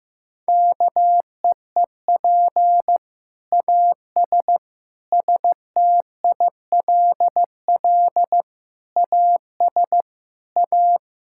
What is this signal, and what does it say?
Morse code, heard as a keyed tone, about 16 wpm, 700 Hz. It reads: KEEP AS STILL AS A
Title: Sentences from Top 1000 Words - Review Part 5 - 15wpm